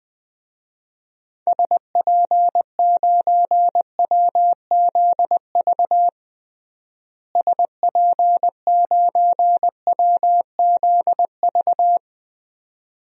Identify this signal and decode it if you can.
Morse code, heard as a keyed tone, about 21 wpm, 700 Hz. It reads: SP9WZV SP9WZV
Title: DXCC-100 Call Signs - 6 Characters - Repeated 2x - 20wpm